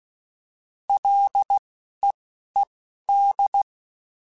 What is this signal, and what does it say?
Morse code, heard as a keyed tone, about 16 wpm, spaced as for 11 wpm, 800 Hz. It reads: LEED
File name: Ls / Single L m3s / Single L Prefixe 3x2 Twice @10